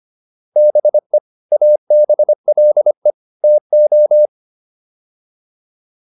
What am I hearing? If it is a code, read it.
Morse code, heard as a keyed tone, about 25 wpm, 600 Hz. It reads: BE ABLE TO
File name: WD205 25